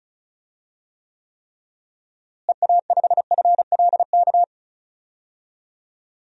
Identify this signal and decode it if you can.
Morse code, heard as a keyed tone, about 36 wpm, 700 Hz. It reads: EA5FLK
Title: DXCC-100 Call Signs - 6 Characters - Repeated 3x - 35wpm